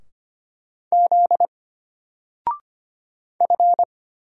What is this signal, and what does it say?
Morse code, heard as a keyed tone, about 25 wpm, 700 Hz. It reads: Z F
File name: Single-Letter-Number-25wpm